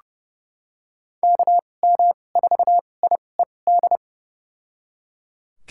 Morse code, heard as a keyed tone, about 30 wpm, spaced as for 22 wpm, 700 Hz. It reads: KM4IED
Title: USA Call Signs - 6 Characters - 2x Character Spacing - 30wpm